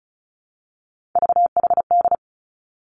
Morse code, heard as a keyed tone, about 36 wpm, 700 Hz. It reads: VHD
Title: Three Challenging Letter-Number Combinations 35wpm